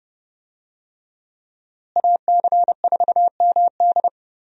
Morse code, heard as a keyed tone, about 30 wpm, 700 Hz. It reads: AC4MD
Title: USA Call Signs - 5 Characters - Repeated 3x - 30wpm